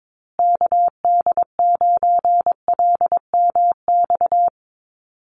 Morse code, heard as a keyed tone, about 22 wpm, 700 Hz. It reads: KD9LMX
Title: USA Call Signs Repeated 3x 22wpm